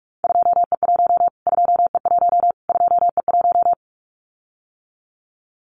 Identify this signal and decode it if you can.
Morse code, heard as a keyed tone, about 44 wpm, 700 Hz. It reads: 2E1 2E1 2E1
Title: Callsign Prefixes for Top 25 DXCC - Repeated 3x - 45wpm